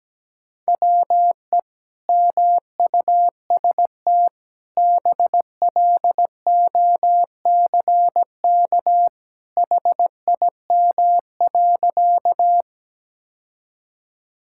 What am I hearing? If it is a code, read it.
Morse code, heard as a keyed tone, about 17 wpm, 700 Hz. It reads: WE MUST BLOCK HIM.
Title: Sentences from Top 1600 Words 17wpm